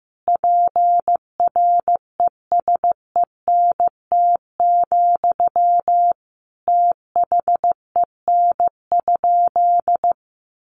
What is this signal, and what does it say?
Morse code, heard as a keyed tone, about 15 wpm, 700 Hz. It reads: PRESENT, THEN?
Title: Sentences from Top 800 Words 15wpm